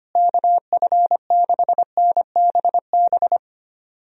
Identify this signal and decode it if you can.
Morse code, heard as a keyed tone, about 25 wpm, 700 Hz. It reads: KF6NBB
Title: USA Call Signs - 6 Characters - Repeated 3x - 25wpm